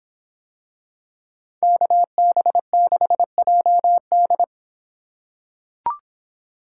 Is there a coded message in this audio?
Yes: Morse code from 1.6 to 4.4 s, heard as a keyed tone, about 26 wpm, 700 Hz. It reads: KB6JD